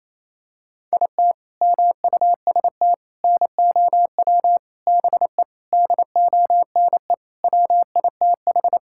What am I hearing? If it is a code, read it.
Morse code, heard as a keyed tone, about 28 wpm, 700 Hz. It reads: IT MUST NOW BE DONE WITH
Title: Sentences from Top 1000 Words - Review Part 5 - 28wpm